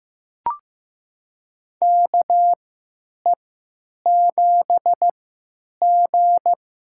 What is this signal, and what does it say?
Morse code, heard as a keyed tone, about 16 wpm, spaced as for 9 wpm, 700 Hz. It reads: KE7G
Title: USA Call Signs - 4 Characters - 3x Character Spacing - 15wpm